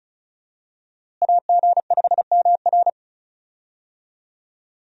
Morse code, heard as a keyed tone, about 36 wpm, 700 Hz. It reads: AG5MR
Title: USA Call Signs - 5 Characters - Repeated 3x - 35wpm